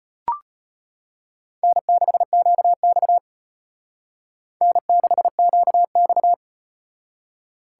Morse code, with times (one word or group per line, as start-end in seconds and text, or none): 1.6-3.2 N6QX
4.6-6.3 N6QX